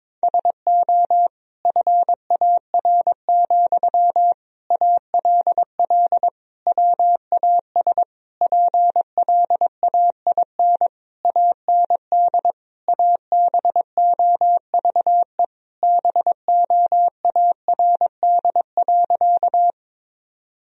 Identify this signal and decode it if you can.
Morse code, heard as a keyed tone, about 22 wpm, 700 Hz. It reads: SO FAR, ALL WAS PLAIN AND ABOVE BOARD.
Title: Sentences from Top 1000 Words - Review Part 4 - 22wpm